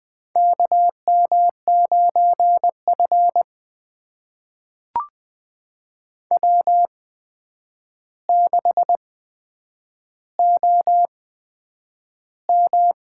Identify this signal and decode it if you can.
Morse code, heard as a keyed tone, about 20 wpm, spaced as for 9 wpm, 700 Hz. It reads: KM9F W6OM
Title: USA Call Signs - 4 Characters - 8x Character Spacing - 20wpm